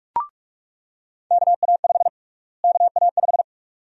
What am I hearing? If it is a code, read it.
Morse code, heard as a keyed tone, about 45 wpm, 700 Hz. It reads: KA5 KA5
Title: USA Call Sign Prefixes - Repeated 2x - 45wpm